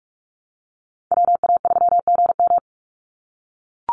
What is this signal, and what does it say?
Morse code, heard as a keyed tone, about 44 wpm, 700 Hz. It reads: WA3GM
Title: Call Signs - 45wpm